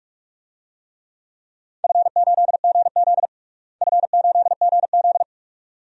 Morse code, heard as a keyed tone, about 46 wpm, 700 Hz. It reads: F8GZ F8GZ
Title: DXCC-100 Call Signs - 4 Characters - Repeated 2x - 45wpm